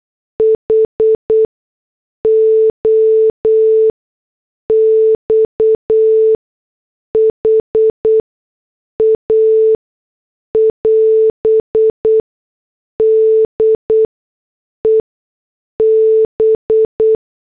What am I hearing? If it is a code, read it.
Morse code, heard as a keyed tone, about 8 wpm, spaced as for 6 wpm, 440 Hz. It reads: HOXHA <AS> DEB
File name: morse / headlines